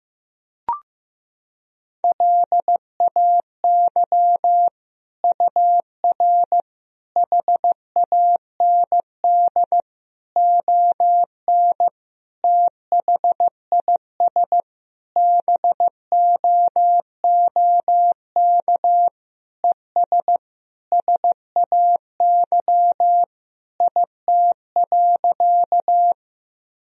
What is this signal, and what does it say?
Morse code, heard as a keyed tone, about 15 wpm, 700 Hz. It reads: LAY UR HAND ON THIS BOOK ES SAY IT.
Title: Sentences from Top 400 Words and Abbreviations 15wpm